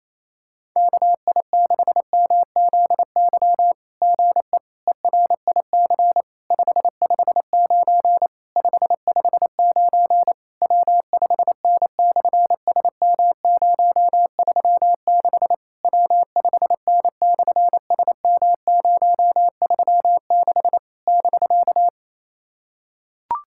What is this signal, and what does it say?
Morse code, heard as a keyed tone, about 27 wpm, 700 Hz. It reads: KI6MZY GE ERIC 559 559 W5N/SM036 W5N/SM036 <BK>
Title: SOTA QSO Practice 28wpm